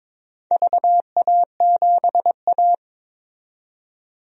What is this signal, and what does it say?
Morse code, heard as a keyed tone, about 22 wpm, 700 Hz. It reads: VA7A
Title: DXCC-100 Call Signs - 4 Characters - 22wpm